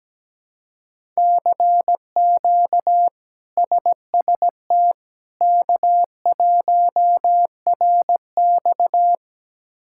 Morse code, heard as a keyed tone, about 17 wpm, 700 Hz. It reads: CQ SST K1RX
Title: K1USN SST QSO Practice - 17wpm